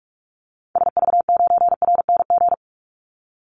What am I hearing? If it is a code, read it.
Morse code, heard as a keyed tone, about 44 wpm, 700 Hz. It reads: SV9RNG
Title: Call Signs Repeated 3x - 45wpm